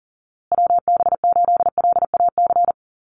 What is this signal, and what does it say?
Morse code, heard as a keyed tone, about 40 wpm, 700 Hz. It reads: WB8LAC
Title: USA Call Signs Repeated 2x 40wpm